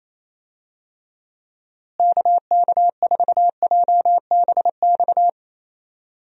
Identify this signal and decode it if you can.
Morse code, heard as a keyed tone, about 28 wpm, 700 Hz. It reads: KK4JBX
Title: USA Call Signs - 6 Characters - Repeated 2x - 28wpm